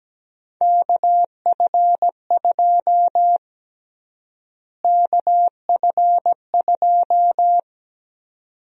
Morse code, heard as a keyed tone, about 17 wpm, 700 Hz. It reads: KF2 KF2
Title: USA Call Sign Prefixes - Repeated 2x - 17wpm